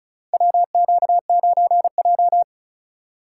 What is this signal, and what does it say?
Morse code, heard as a keyed tone, about 34 wpm, 700 Hz. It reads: WQ9J
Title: USA Call Signs - 4 Characters - 4x Character Spacing - 35wpm